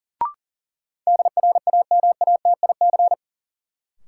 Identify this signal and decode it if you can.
Morse code, heard as a keyed tone, about 40 wpm, 700 Hz. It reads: DRAMATIC
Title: Sentences from Top 1300 Words - New Word Drill Alt - 40wpm